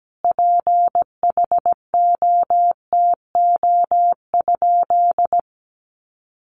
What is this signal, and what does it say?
Morse code, heard as a keyed tone, about 17 wpm, 700 Hz. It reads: PHOTO?
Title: Sentences from Top 1200 Words 17wpm